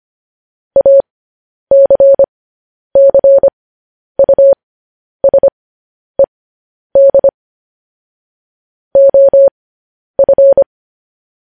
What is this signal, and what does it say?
Morse code, heard as a keyed tone, about 25 wpm, spaced as for 10 wpm, 550 Hz. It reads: ACCUSED OF
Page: news headlines in morse code at 10 wpm